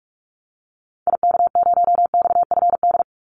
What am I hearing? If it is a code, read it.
Morse code, heard as a keyed tone, about 44 wpm, 700 Hz. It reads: IK0XFD
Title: Call Signs Repeated 3x - 45wpm